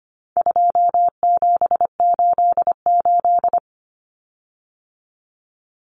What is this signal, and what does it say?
Morse code, heard as a keyed tone, about 25 wpm, 700 Hz. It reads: 2788
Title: Common Cut Numbers - 4-Digit 25wpm